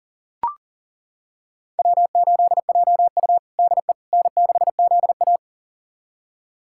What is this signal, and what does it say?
Morse code, heard as a keyed tone, about 40 wpm, 700 Hz. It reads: W8JU DE N6ZA
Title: USA Callsign DE Callsign - 4 Characters - 40wpm